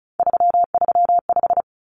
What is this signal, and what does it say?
Morse code, heard as a keyed tone, about 34 wpm, 700 Hz. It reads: 335
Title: Numbers - 3-Digit 35wpm